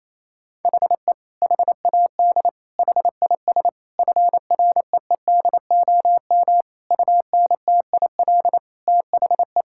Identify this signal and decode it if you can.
Morse code, heard as a keyed tone, about 28 wpm, 700 Hz. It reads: HE HAD HIS FREEDOM UNTIL THE
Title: Sentences from Top 1000 Words - Review Part 5 - 28wpm